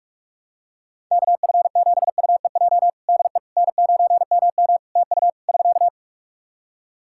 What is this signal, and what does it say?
Morse code, heard as a keyed tone, about 46 wpm, 700 Hz. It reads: KF7UEJ DE N9MM TU <SK>
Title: Practice QSO - Exchange 4 - Ending - 45wpm